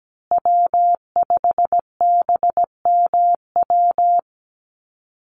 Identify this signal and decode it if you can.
Morse code, heard as a keyed tone, about 17 wpm, 700 Hz. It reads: W5BMW
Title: USA Call Signs Repeated 2x 17wpm